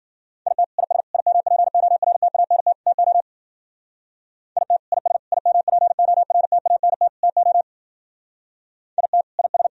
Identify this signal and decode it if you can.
Morse code, heard as a keyed tone, about 50 wpm, 700 Hz. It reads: IT IS IMPORTANT TO IT IS IMPORTANT TO IT IS
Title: Speed-Racing - Sets of 4 Words 40wpm